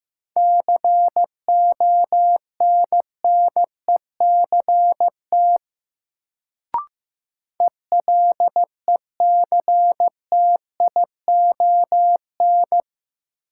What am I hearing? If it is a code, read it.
Morse code, heard as a keyed tone, about 15 wpm, 700 Hz. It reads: CONNECT ELECTION
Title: Sentences from Top 1500 Words - New Word Drill - 15wpm